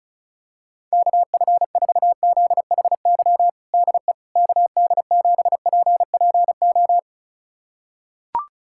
Morse code, heard as a keyed tone, about 34 wpm, 700 Hz. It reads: KF4ZHY DE KD7PPO
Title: USA Callsign DE Callsign 35wpm